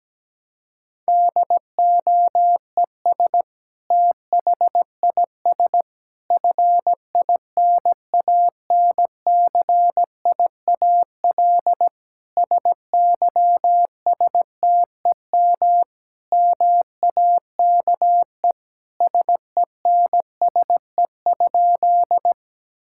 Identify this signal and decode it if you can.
Morse code, heard as a keyed tone, about 17 wpm, 700 Hz. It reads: DOES THIS FINANCIAL SYSTEM MAKE SENSE?
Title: Sentences from Top 1000 Words - Review Part 5 - 17wpm